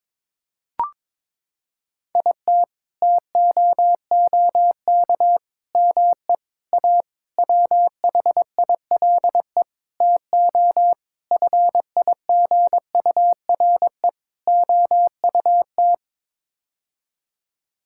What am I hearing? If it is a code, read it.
Morse code, heard as a keyed tone, about 22 wpm, 700 Hz. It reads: IT TOOK ME A WHILE TO FIGURE OUT
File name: practice / 22wpm / Sets of 8 Words - 22wpm